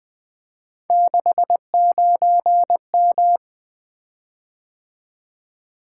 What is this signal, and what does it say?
Morse code, heard as a keyed tone, about 20 wpm, 700 Hz. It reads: 69M
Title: Three Letter-Number Combinations - 20wpm